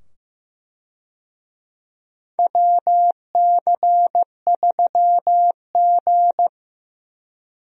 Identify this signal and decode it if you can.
Morse code, heard as a keyed tone, about 15 wpm, 700 Hz. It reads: WC3G
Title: USA Call Signs - 4 Characters - 15wpm